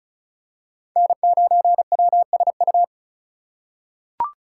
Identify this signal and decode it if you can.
Morse code, heard as a keyed tone, about 36 wpm, 700 Hz. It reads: N9WSU
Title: USA Call Signs - 5 Characters - 4x Character Spacing - 35wpm